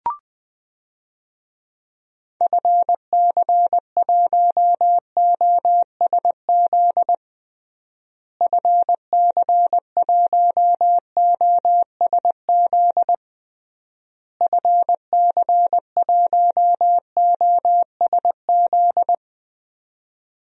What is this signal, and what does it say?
Morse code, heard as a keyed tone, about 21 wpm, 700 Hz. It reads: FC1OSZ FC1OSZ FC1OSZ
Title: DXCC-100 Call Signs - 6 Characters - Repeated 3x - 20wpm